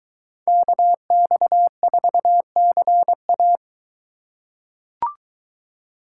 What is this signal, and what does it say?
Morse code, heard as a keyed tone, about 23 wpm, 700 Hz. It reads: KX4CA